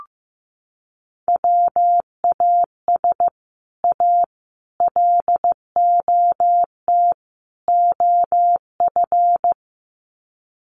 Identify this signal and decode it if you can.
Morse code, heard as a keyed tone, about 15 wpm, 700 Hz. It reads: WAS A LOT OF
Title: Sets of 4 Words 15wpm